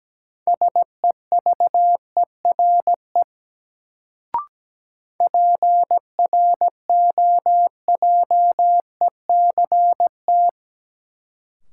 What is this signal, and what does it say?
Morse code, heard as a keyed tone, about 17 wpm, 700 Hz. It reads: SEVERE PROJECT
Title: Sentences from Top 1400 Words - New Word Drill - 17wpm